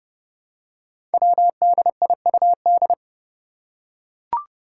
Morse code, heard as a keyed tone, about 30 wpm, 700 Hz. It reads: WDIUD
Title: Code Groups - 5 Letter:Numbers - 30wpm